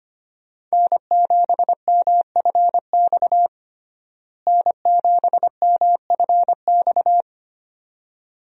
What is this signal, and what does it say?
Morse code, heard as a keyed tone, about 25 wpm, 700 Hz. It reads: N7MFX N7MFX
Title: USA Call Signs - 5 Characters - Repeated 2x - 25wpm